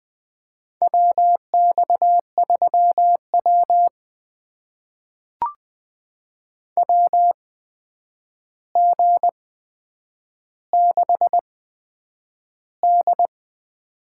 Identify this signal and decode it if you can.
Morse code, heard as a keyed tone, about 20 wpm, spaced as for 9 wpm, 700 Hz. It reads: WX3W WG6D
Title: USA Call Signs - 4 Characters - 8x Character Spacing - 20wpm